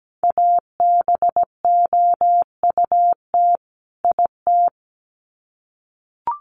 Morse code, heard as a keyed tone, about 17 wpm, 700 Hz. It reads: ABOUT IT
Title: Sets of 2 Words 17wpm